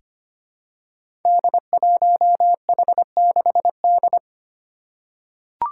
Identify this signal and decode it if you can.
Morse code, heard as a keyed tone, about 25 wpm, 700 Hz. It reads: D1H6D
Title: Five Challenging Letter-Number Combinations 25wpm